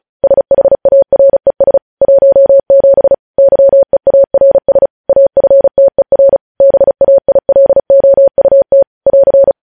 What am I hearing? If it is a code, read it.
Morse code, heard as a keyed tone, about 36 wpm, 550 Hz. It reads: SHARES 17 YEARS AFTER BAILOUT <AR>
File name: podcast / News headlines-1x-35x35